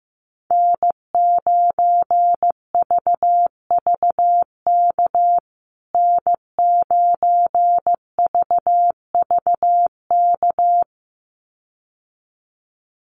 Morse code, heard as a keyed tone, about 15 wpm, 700 Hz. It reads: N9VVK N9VVK
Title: USA Call Signs Repeated 2x 15wpm